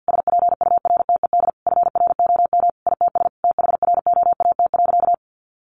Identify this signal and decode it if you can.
Morse code, heard as a keyed tone, about 49 wpm, 700 Hz. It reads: SPURTED FROM ITS THROAT.
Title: A Sound Of Thunder 50wpm